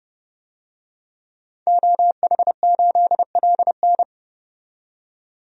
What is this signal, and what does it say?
Morse code, heard as a keyed tone, about 30 wpm, 690 Hz. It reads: OH8LN
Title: DXCC-100 Call Signs - 5 Characters - 30wpm